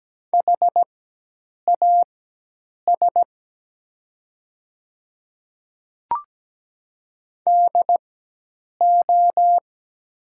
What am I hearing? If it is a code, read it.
Morse code, heard as a keyed tone, about 17 wpm, spaced as for 8 wpm, 700 Hz. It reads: HAS DO
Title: 4x Character Spacing - Top 100 Words - 17wpm